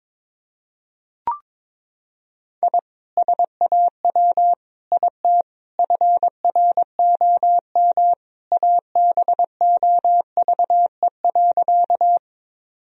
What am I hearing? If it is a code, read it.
Morse code, heard as a keyed tone, about 22 wpm, 700 Hz. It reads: I SAW IT FROM ABOVE.
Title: Sentences from Top 1000 Words - Review Part 2 - 22wpm